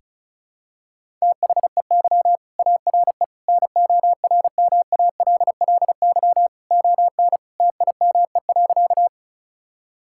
Morse code, heard as a keyed tone, about 36 wpm, 700 Hz. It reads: THEY ARE NORMALLY ON TIME.
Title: Sentences from Top 1000 Words - Review Part 6 - 35wpm